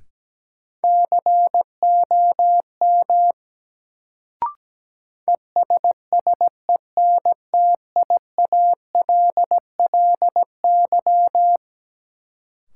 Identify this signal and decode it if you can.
Morse code, heard as a keyed tone, about 17 wpm, 700 Hz. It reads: COM ESSENTIALLY
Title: Sentences from Top 1300 Words - New Word Drill - 17wpm